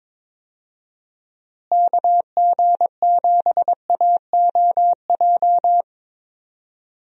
Morse code, heard as a keyed tone, about 22 wpm, 700 Hz. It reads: KG7AOJ
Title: USA Call Signs - 6 Characters - Repeated 3x - 22wpm